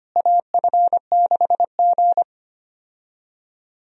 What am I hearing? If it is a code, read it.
Morse code, heard as a keyed tone, about 25 wpm, 700 Hz. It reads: AF6G
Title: USA Call Signs - 4 Characters - Repeated 3x - 25wpm